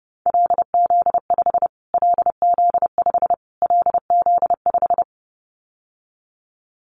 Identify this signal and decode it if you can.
Morse code, heard as a keyed tone, about 30 wpm, 700 Hz. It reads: LZ5 LZ5 LZ5
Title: Callsign Prefixes for Top 25 DXCC - Repeated 3x - 30wpm